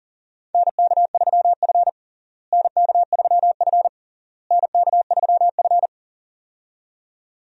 Morse code, heard as a keyed tone, about 37 wpm, 700 Hz. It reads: NK3F NK3F NK3F
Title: USA Call Signs - 4 Characters - Repeated 3x - 40wpm